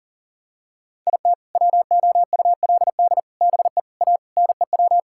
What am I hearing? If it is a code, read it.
Morse code, heard as a keyed tone, about 40 wpm, 700 Hz. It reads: IT WOULD BE A NEW
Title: Sentences from Top 1000 Words - Review Part 5 - 40wpm